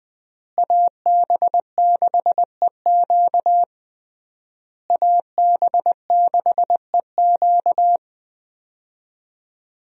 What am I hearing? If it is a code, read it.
Morse code, heard as a keyed tone, about 21 wpm, 700 Hz. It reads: AB6EQ AB6EQ
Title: USA Call Signs - 5 Characters - Repeated 3x - 20wpm